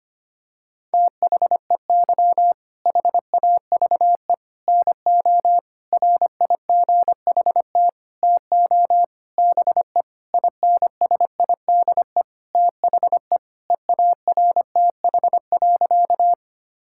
Morse code, heard as a keyed tone, about 25 wpm, 700 Hz. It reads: THEY HAVE NO RIGHT TO BE INSIDE THE EARTH.
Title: Sentences from Top 1000 Words - Review Part 6 - 25wpm